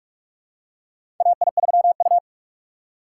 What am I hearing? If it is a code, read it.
Morse code, heard as a keyed tone, about 46 wpm, 700 Hz. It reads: AI3U
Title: USA Call Signs - 4 Characters - 2x Character Spacing - 45wpm